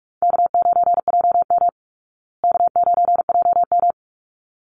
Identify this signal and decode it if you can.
Morse code, heard as a keyed tone, about 44 wpm, 700 Hz. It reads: K9JM K9JM
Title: Easy Call Signs Repeated 2x - 45wpm